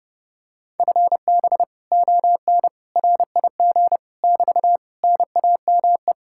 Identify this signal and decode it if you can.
Morse code, heard as a keyed tone, about 31 wpm, 700 Hz. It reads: FB ON RIG <BT> NAME